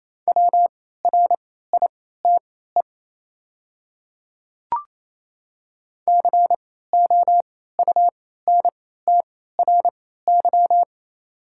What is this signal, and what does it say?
Morse code, heard as a keyed tone, about 28 wpm, spaced as for 16 wpm, 690 Hz. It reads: WRITE COUNTRY
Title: 3x Character Spacing - Top 500 Words - 28wpm